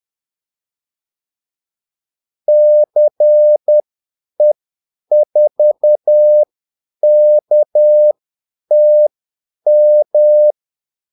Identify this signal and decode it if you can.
Morse code, heard as a keyed tone, about 10 wpm, spaced as for 8 wpm, 600 Hz. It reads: CE4KTM
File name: Callsigns-10-8wpm